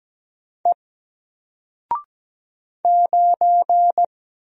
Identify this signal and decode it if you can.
Morse code, heard as a keyed tone, about 17 wpm, 700 Hz. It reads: E 9
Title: Single Letter-Number Alt 17wpm